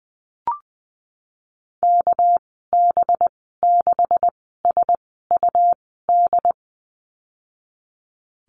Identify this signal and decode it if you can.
Morse code, heard as a keyed tone, about 20 wpm, spaced as for 14 wpm, 700 Hz. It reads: KB6SUD
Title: 219 Lesson 35g - Number 0 - Call Signs 20wpm